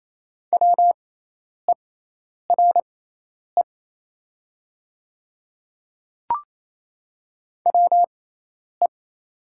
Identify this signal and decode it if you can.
Morse code, heard as a keyed tone, about 28 wpm, spaced as for 10 wpm, 700 Hz. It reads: WERE WE
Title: 6x Character Spacing - Top 200 Words - 28wpm